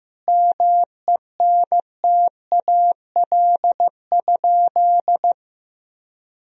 Morse code, heard as a keyed tone, about 15 wpm, 700 Hz. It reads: MENTAL?
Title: Sentences from Top 1000 Words - Review Part 4 - 15wpm